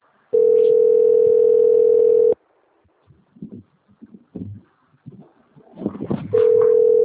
Ringback tone (North America), 440 + 480 Hz.